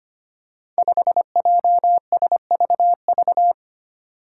Morse code, heard as a keyed tone, about 25 wpm, 700 Hz. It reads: 5JSVV
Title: Five Challenging Letter-Number Combinations 25wpm